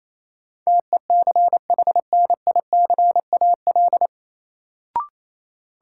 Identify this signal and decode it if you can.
Morse code, heard as a keyed tone, about 28 wpm, 700 Hz. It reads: TECHNICAL
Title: Sentences from Top 1100 Words - New Word Drill - 28wpm